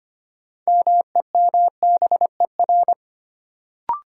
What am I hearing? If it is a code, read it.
Morse code, heard as a keyed tone, about 25 wpm, 700 Hz. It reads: MEMBER
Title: Sentences from Top 1100 Words - New Word Drill Alt - 25wpm